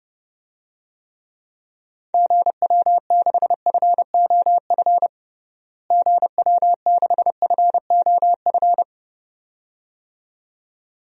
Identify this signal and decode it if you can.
Morse code, heard as a keyed tone, about 31 wpm, 700 Hz. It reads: GW6FOF GW6FOF
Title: DXCC-100 Call Signs - 6 Characters - Repeated 2x - 30wpm